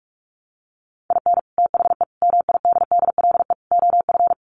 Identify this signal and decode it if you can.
Morse code, heard as a keyed tone, about 45 wpm, 700 Hz. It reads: IN THE MIDDLE OF
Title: Sets of 4 Words 45wpm